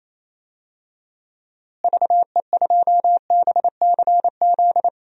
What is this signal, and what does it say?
Morse code, heard as a keyed tone, about 28 wpm, 700 Hz. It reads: VE2BCZ
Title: DXCC-100 Call Signs - 6 Characters - Repeated 2x - 28wpm